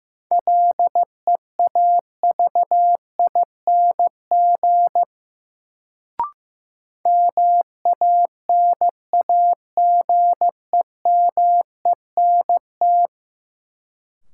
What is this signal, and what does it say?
Morse code, heard as a keyed tone, about 15 wpm, 700 Hz. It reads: LEAVING MANAGEMENT
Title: Sentences from Top 1000 Words - Review Part 4 - New Word Drill - 15wpm